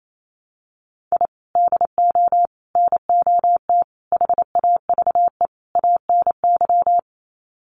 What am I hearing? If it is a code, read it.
Morse code, heard as a keyed tone, about 28 wpm, 700 Hz. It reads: I DO NOT HAVE ANY
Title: Sets of 5 Words 28wpm